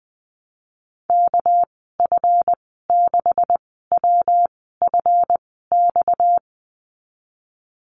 Morse code, heard as a keyed tone, about 20 wpm, spaced as for 14 wpm, 700 Hz. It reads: KF6WFX
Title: 238 Lesson 38g - Letter q - Call Signs 20wpm